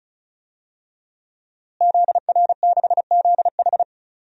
Morse code, heard as a keyed tone, about 36 wpm, 700 Hz. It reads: ZR6ZH
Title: DXCC-100 Call Signs - 5 Characters - Repeated 3x - 35wpm